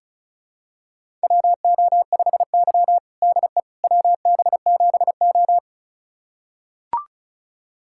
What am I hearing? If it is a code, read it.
Morse code, heard as a keyed tone, about 36 wpm, 700 Hz. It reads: WO5Y DE WB7O
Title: USA Callsign DE Callsign - 4 Characters - 35wpm